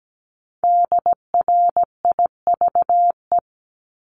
Morse code, heard as a keyed tone, about 17 wpm, 700 Hz. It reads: DRIVE
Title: Top 500 Words Repeated 17wpm